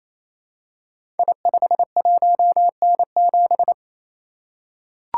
Morse code, heard as a keyed tone, about 28 wpm, 700 Hz. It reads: I51N7
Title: Code Groups - 5 Letter:Numbers - 28wpm